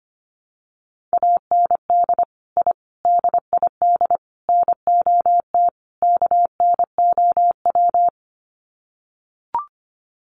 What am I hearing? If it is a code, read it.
Morse code, heard as a keyed tone, about 25 wpm, 700 Hz. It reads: AND I DID NOT KNOW
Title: Sets of 5 Words Encore 25wpm